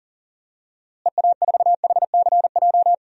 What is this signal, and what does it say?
Morse code, heard as a keyed tone, about 40 wpm, 700 Hz. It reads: EA4HCJ
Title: DXCC-100 Call Signs - 6 Characters - Repeated 2x - 40wpm